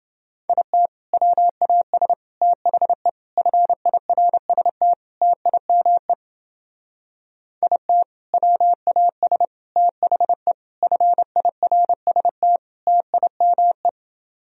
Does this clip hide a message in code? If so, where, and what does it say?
Yes: Morse code from 0.5 to 13.9 s, heard as a keyed tone, about 28 wpm, 700 Hz. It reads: IT WAS THE FIRST TIME IT WAS THE FIRST TIME